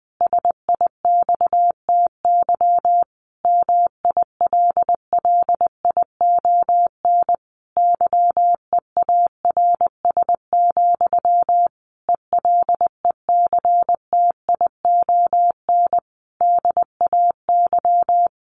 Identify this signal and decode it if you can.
Morse code, heard as a keyed tone, about 20 wpm, 700 Hz. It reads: SIXTY MILLION YEARS, ELECTION DAY
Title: A Sound Of Thunder 20wpm